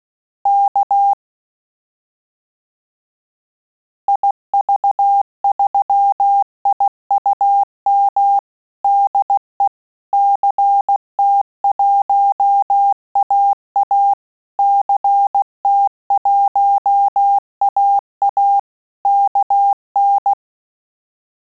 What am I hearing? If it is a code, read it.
Morse code, heard as a keyed tone, about 16 wpm, 800 Hz. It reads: K IV3IUM DE CT1AA CT1AA KN
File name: QSO CW 75 CPM